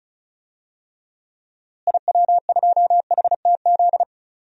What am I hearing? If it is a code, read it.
Morse code, heard as a keyed tone, about 36 wpm, 700 Hz. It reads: IW2HTZ
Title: DXCC-100 Call Signs - 6 Characters - Repeated 3x - 35wpm